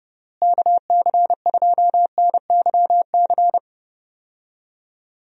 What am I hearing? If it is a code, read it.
Morse code, heard as a keyed tone, about 32 wpm, 700 Hz. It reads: KC2NYC
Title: USA Call Signs - 6 Characters - Repeated 3x - 30wpm